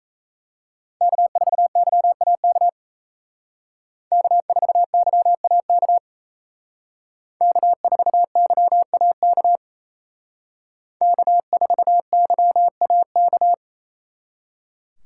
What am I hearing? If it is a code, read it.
Morse code, heard as a keyed tone, about 34 wpm, 700 Hz. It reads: K4YAK K4YAK K4YAK K4YAK
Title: Speed Racing - USA 1x3 Call Signs 28wpm